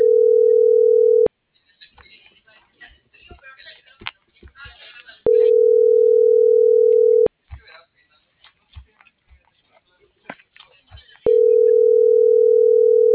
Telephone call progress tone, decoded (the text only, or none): ringback tone